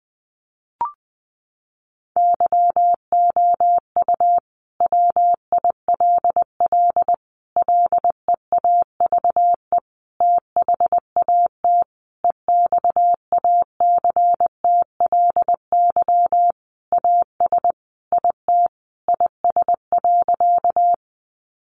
Morse code, heard as a keyed tone, about 20 wpm, 700 Hz. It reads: YOU WILL LEAVE THAT EXACTLY AS IT IS.
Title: Sentences from Top 900 Words 20wpm